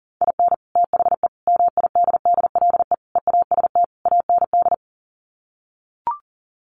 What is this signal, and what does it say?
Morse code, heard as a keyed tone, about 40 wpm, 700 Hz. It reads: IN THE MIDDLE EAST AND
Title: Sets of 5 Words Encore 40wpm